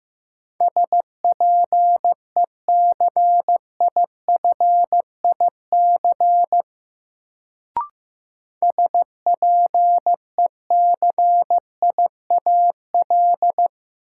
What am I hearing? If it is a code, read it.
Morse code, heard as a keyed tone, about 15 wpm, 700 Hz. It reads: SPECIFIC SPECIAL
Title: Sentences from Top 1000 Words - Review Part 3 - New Word Drill - 15wpm